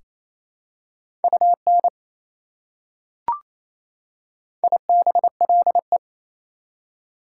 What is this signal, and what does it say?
Morse code, heard as a keyed tone, about 28 wpm, 700 Hz. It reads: UN IBLE
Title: Prefixes and Suffixes 28wpm